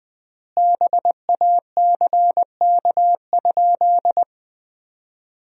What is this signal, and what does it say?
Morse code, heard as a keyed tone, about 20 wpm, 700 Hz. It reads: BACK?
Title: 3x Word-Spacing - Sentences from Top 400 Words - 20wpm